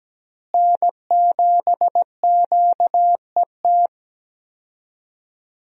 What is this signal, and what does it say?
Morse code, heard as a keyed tone, about 17 wpm, 700 Hz. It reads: N7QET
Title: USA Call Signs - 5 Characters - Repeated 2x - 17wpm